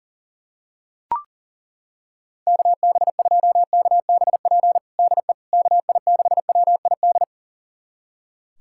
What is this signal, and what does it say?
Morse code, heard as a keyed tone, about 40 wpm, 700 Hz. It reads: KB2KBP DE KI6WID
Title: USA Callsign DE Callsign - 6 Characters - 40wpm